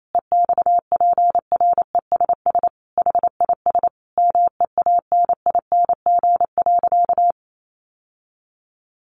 Morse code, heard as a keyed tone, about 28 wpm, 700 Hz. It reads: EXPRESS HIS MEANING.
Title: Sentences from Top 1100 Words 28wpm